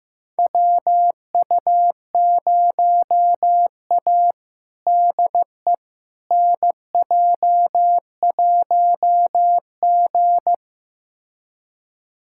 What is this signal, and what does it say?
Morse code, heard as a keyed tone, about 15 wpm, 700 Hz. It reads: WU0A DE NJ1G
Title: USA Callsign DE Callsign - 4 Characters - 15wpm